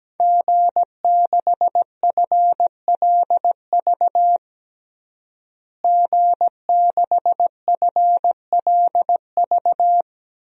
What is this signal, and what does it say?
Morse code, heard as a keyed tone, about 17 wpm, 700 Hz. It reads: G6FLV G6FLV
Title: DXCC-100 Call Signs - 5 Characters - Repeated 3x - 17wpm